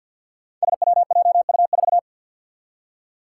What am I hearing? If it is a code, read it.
Morse code, heard as a keyed tone, about 48 wpm, 700 Hz. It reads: SWJU4
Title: Five Challenging Letter-Number Combinations 50wpm